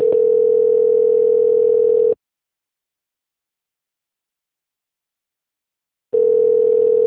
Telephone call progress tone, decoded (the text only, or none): ringback tone